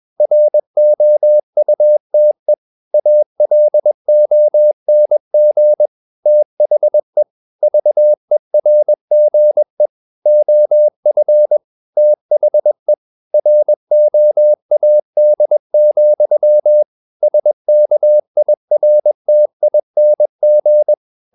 Morse code, heard as a keyed tone, about 21 wpm, 600 Hz. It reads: ROUTE ALONG THE VERGE OF THE ROAD, SKIRTING
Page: Romanen Never af Ken Follet på engelsk.